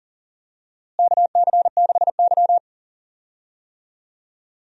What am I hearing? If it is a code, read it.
Morse code, heard as a keyed tone, about 38 wpm, 700 Hz. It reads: KC6Y
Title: USA Call Signs - 4 Characters - Repeated 3x - 40wpm